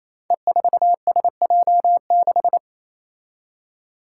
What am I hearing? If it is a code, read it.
Morse code, heard as a keyed tone, about 28 wpm, 700 Hz. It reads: E4SJ6
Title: Five Challenging Letter-Number Combinations 28wpm